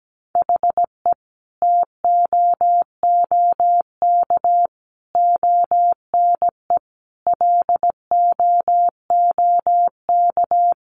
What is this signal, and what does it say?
Morse code, heard as a keyed tone, about 17 wpm, 700 Hz. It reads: HE TOOK ONE LOOK
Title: Sentences from Top 600 Words 17wpm